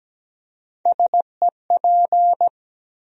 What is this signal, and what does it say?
Morse code, heard as a keyed tone, about 17 wpm, 700 Hz. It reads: SEP
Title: USA Call Sign Postfixes - 3 Letters 17wpm